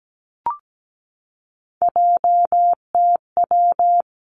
Morse code, heard as a keyed tone, about 17 wpm, 700 Hz. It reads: JTW
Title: Three Challenging Letter-Number Combinations 17wpm